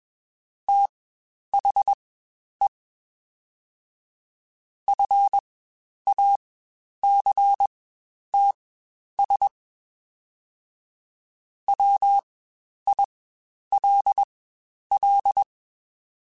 Morse code, heard as a keyed tone, about 21 wpm, spaced as for 10 wpm, 800 Hz. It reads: THE FACTS WILL
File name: Quotes Spaced / Quotes Spaced-mp3s / IgnoringFacts sp @20